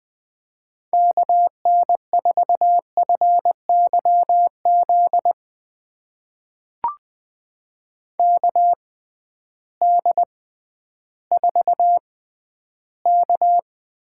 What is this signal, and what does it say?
Morse code, heard as a keyed tone, about 20 wpm, spaced as for 12 wpm, 700 Hz. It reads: KN4FYZ KD4K
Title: USA Call Signs - 6 Characters - 6x Character Spacing - 20wpm